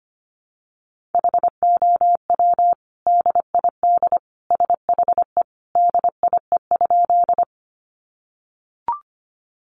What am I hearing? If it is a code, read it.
Morse code, heard as a keyed tone, about 25 wpm, 700 Hz. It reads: HOW DID SHE DIE?
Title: Sentences from Top 1300 Words 25wpm